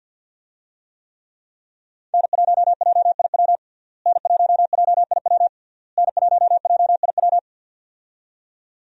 Morse code, heard as a keyed tone, about 48 wpm, 700 Hz. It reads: N1JIW N1JIW N1JIW
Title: USA Call Signs - 5 Characters - Repeated 3x - 50wpm